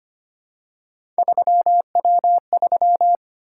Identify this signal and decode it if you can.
Morse code, heard as a keyed tone, about 25 wpm, 700 Hz. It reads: 3W3
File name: Three Letter-Number Combinations - 25wpm